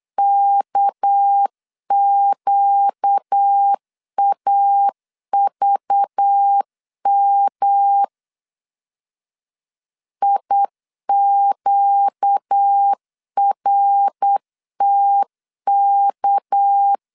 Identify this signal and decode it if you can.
Morse code, heard as a keyed tone, about 8 wpm, 800 Hz. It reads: KQAVM IQRTK
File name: LEZIONI / 6 PKX0 / 6 PK 8wpm 3003